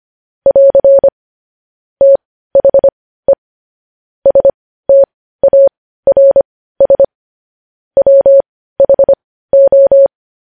Morse code, heard as a keyed tone, about 25 wpm, spaced as for 15 wpm, 550 Hz. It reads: <AR> THE STARS WHO
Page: news headlines in morse code at 15 wpm